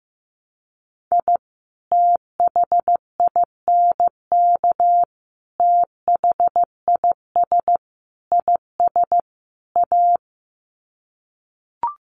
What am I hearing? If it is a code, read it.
Morse code, heard as a keyed tone, about 15 wpm, 700 Hz. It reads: I THINK THIS IS A
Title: Sets of 5 Words 15wpm